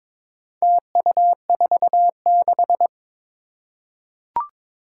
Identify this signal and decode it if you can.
Morse code, heard as a keyed tone, about 22 wpm, 700 Hz. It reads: TU46
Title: Four Challenging Letter-Number Combinations 22wpm